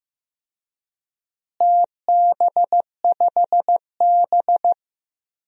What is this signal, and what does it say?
Morse code, heard as a keyed tone, about 16 wpm, 700 Hz. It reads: TB5B
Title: DXCC-100 Call Signs - 4 Characters - Repeated 3x - 15wpm